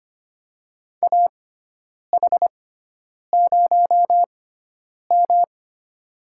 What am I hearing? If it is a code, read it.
Morse code, heard as a keyed tone, about 25 wpm, spaced as for 9 wpm, 700 Hz. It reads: AH0M
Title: USA Call Signs - 4 Characters - 6x Character Spacing - 25wpm